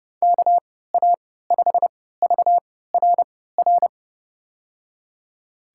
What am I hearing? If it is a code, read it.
Morse code, heard as a keyed tone, about 31 wpm, spaced as for 17 wpm, 700 Hz. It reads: KA5VRR
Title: USA Call Signs - 6 Characters - 3x Character Spacing - 30wpm